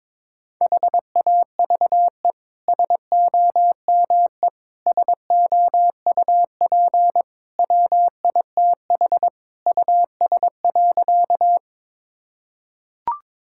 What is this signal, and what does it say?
Morse code, heard as a keyed tone, about 22 wpm, 700 Hz. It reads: HAVE SOME SOUP WITH US.
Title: Sentences from Top 1000 Words - Review Part 6 - 22wpm